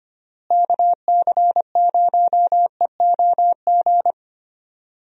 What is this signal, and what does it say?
Morse code, heard as a keyed tone, about 25 wpm, 700 Hz. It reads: KC0EOG
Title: USA Call Signs - 6 Characters - Repeated 3x - 25wpm